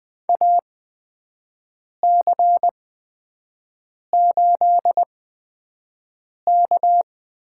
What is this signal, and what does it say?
Morse code, heard as a keyed tone, about 20 wpm, spaced as for 5 wpm, 700 Hz. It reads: AC8K
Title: USA Call Signs - 5 Characters - 8x Character Spacing - 20wpm